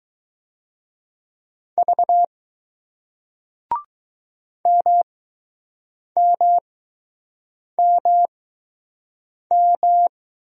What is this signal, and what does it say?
Morse code, heard as a keyed tone, about 20 wpm, spaced as for 6 wpm, 700 Hz. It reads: V MMMM